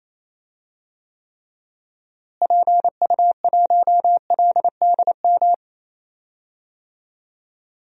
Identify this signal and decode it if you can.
Morse code, heard as a keyed tone, about 28 wpm, 700 Hz. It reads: PU1LDM
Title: DXCC-100 Call Signs - 6 Characters - Repeated 3x - 28wpm